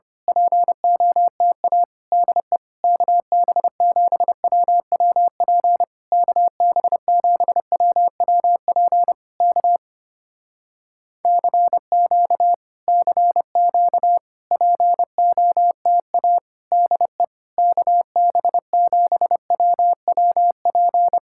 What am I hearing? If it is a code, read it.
Morse code, heard as a keyed tone, about 29 wpm, 700 Hz. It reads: POTA DE KB7WWP KB7WWP K CQ CQ POTA DE KB7WWP
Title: Speed-Racing Common QSO Phrases - 25wpm